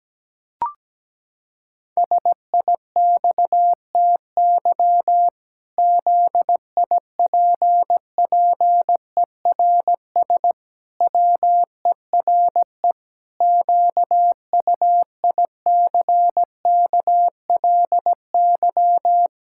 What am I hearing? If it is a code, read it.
Morse code, heard as a keyed tone, about 17 wpm, 700 Hz. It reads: SIXTY ZIPPERS WERE QUICKLY
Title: Pangram Challenge v2 - 17wpm